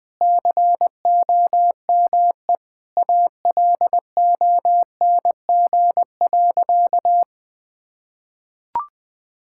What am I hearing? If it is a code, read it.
Morse code, heard as a keyed tone, about 20 wpm, 700 Hz. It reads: COME ALONG.
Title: Sentences from Top 1600 Words 20wpm